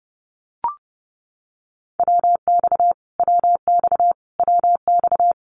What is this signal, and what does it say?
Morse code, heard as a keyed tone, about 28 wpm, 700 Hz. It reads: WX WX WX
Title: Abbreviations Repeated 3x - 30wpm